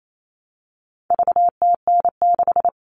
Morse code, heard as a keyed tone, about 28 wpm, 700 Hz. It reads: VTN6
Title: All Cut Numbers - 4-Digit 28wpm